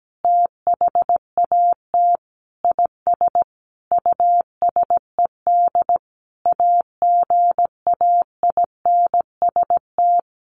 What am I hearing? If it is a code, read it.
Morse code, heard as a keyed tone, about 17 wpm, 700 Hz. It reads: THAT IS USED AGAINST
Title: Sentences from Top 1500 Words 17wpm